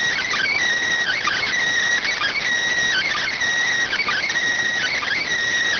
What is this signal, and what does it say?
Emergency Alert System (SAME) tones (header bursts), heard as harsh buzzing bursts. It is heard from 1.0 to 5.3 s.